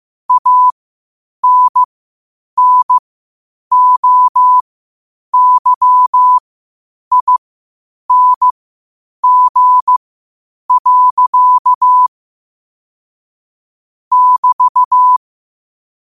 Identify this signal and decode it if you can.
Morse code, heard as a keyed tone, about 15 wpm, spaced as for 8 wpm, 1000 Hz. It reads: ANNOYING. <BT>
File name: qotdfiles / 8wpm